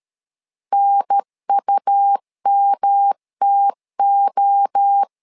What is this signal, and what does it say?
Morse code, heard as a keyed tone, about 13 wpm, 800 Hz. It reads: NUMTO
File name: LEZIONI / 4 Q1S 12wpm 1802